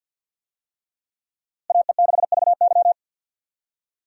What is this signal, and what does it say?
Morse code, heard as a keyed tone, about 47 wpm, 700 Hz. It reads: AE6VY
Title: USA Call Signs - 5 Characters - Repeated 3x - 50wpm